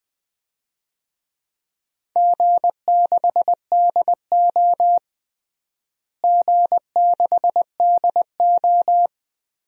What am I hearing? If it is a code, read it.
Morse code, heard as a keyed tone, about 20 wpm, 700 Hz. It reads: G6DO G6DO
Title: DXCC-100 Call Signs - 4 Characters - Repeated 3x - 20wpm